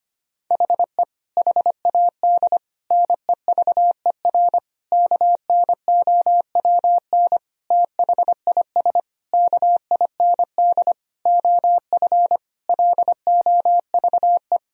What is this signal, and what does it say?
Morse code, heard as a keyed tone, about 25 wpm, 700 Hz. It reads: HE HAD NEVER KNOWN THIS KIND OF LOVE
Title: Sentences from Top 1000 Words - Review Part 3 - 25wpm